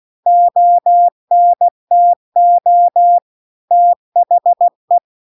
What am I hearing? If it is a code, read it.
Morse code, heard as a keyed tone, about 16 wpm, 700 Hz. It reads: ONTO THE
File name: CW-The-Big-Twoheaded-River-2-16WPM